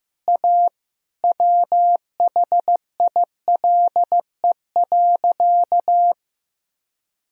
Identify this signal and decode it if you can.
Morse code, heard as a keyed tone, about 15 wpm, 700 Hz. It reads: A WHILE.
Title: Sentences from Top 1000 Words - Review Part 1 - 15wpm